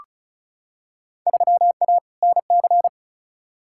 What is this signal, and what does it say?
Morse code, heard as a keyed tone, about 36 wpm, 700 Hz. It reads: 3A NC
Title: Field Day Exchanges Encore - 35wpm